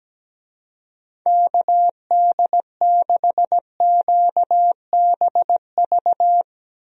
Morse code, heard as a keyed tone, about 17 wpm, 700 Hz. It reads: KD6QBV
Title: USA Call Signs - 6 Characters - Repeated 2x - 17wpm